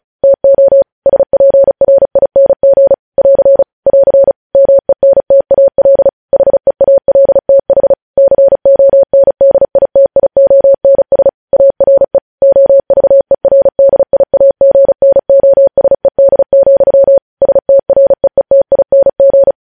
Morse code, heard as a keyed tone, about 36 wpm, 550 Hz. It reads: TO SPRING <AR> <AR> MENTAL HEALTH CONDITIONS ARE OVERDIAGNOSED, STREETING
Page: News Headlines in Morse Code at 35 WPM